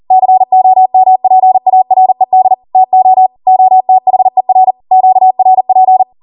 Morse code, heard as a keyed tone, about 40 wpm, 750 Hz. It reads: COMPARED TO OTHER QRP